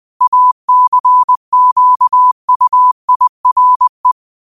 Morse code, heard as a keyed tone, about 20 wpm, 1000 Hz. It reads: ACQUIRE